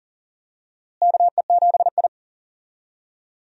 Morse code, heard as a keyed tone, about 40 wpm, 700 Hz. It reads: KE7I
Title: USA Call Signs - 4 Characters - Repeated 2x - 40wpm